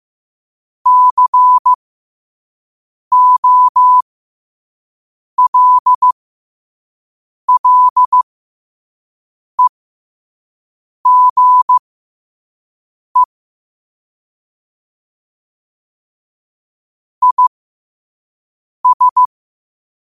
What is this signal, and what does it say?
Morse code, heard as a keyed tone, about 15 wpm, spaced as for 5 wpm, 1000 Hz. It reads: COLLEGE IS
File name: qotdfiles / 5wpm